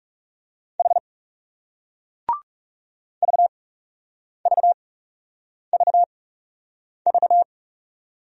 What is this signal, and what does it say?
Morse code, heard as a keyed tone, about 40 wpm, spaced as for 8 wpm, 700 Hz. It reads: H VVVV